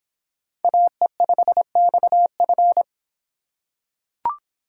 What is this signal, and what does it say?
Morse code, heard as a keyed tone, about 26 wpm, 700 Hz. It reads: AE5XF